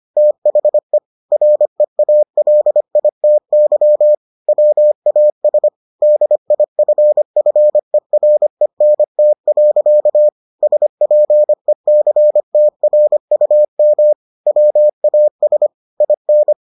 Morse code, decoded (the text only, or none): THE REALITY WAS DIFFERENT. SPECTRUM WAS IN